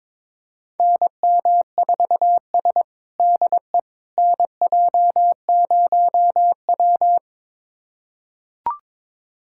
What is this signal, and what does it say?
Morse code, heard as a keyed tone, about 22 wpm, 700 Hz. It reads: NM4S DE NJ0W
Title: USA Callsign DE Callsign - 4 Characters - 22wpm